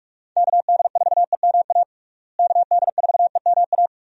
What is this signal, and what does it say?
Morse code, heard as a keyed tone, about 42 wpm, 700 Hz. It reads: KD4EMA KD4EMA
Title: USA Call Signs - 6 Characters - Repeated 2x - 45wpm